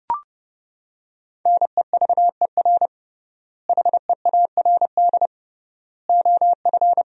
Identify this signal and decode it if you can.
Morse code, heard as a keyed tone, about 32 wpm, 700 Hz. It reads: NEVER HEARD OF
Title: 3x Word-Spacing - Sentences from Top 400 Words - 30wpm